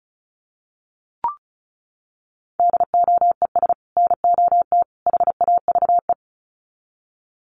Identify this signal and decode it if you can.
Morse code, heard as a keyed tone, about 34 wpm, 700 Hz. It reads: DOES NOT HAVE
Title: Sets of 3 Words 35wpm